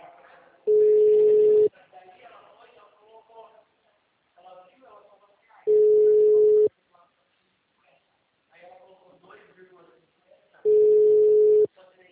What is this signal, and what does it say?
Ringback tone (Europe, 425 Hz), with speech.